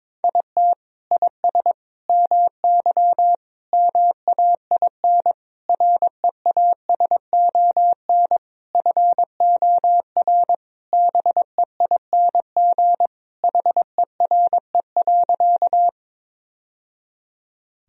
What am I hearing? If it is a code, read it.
Morse code, heard as a keyed tone, about 22 wpm, 700 Hz. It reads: IT IS MY MAIN REASON FOR BEING HERE.
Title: Sentences from Top 1000 Words - Review Part 2 - 22wpm